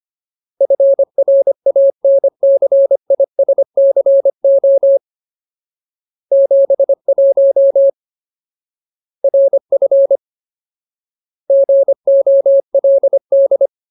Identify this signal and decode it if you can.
Morse code, heard as a keyed tone, about 25 wpm, 550 Hz. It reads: FRANCISCO 71 RF GOLD
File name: Ultimate Word List / 25WPM Ultimate